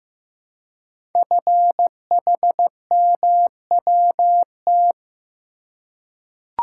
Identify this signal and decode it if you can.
Morse code, heard as a keyed tone, about 15 wpm, 700 Hz. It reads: FHMWT
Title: Code Groups - 5 Letters - 15wpm